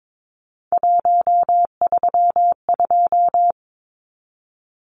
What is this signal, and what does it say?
Morse code, heard as a keyed tone, about 22 wpm, 700 Hz. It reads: 132
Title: Numbers - 3-Digit 22wpm